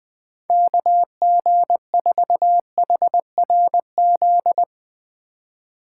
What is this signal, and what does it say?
Morse code, heard as a keyed tone, about 22 wpm, 700 Hz. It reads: KG4HRZ
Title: Speed Racing - USA Call Signs 20wpm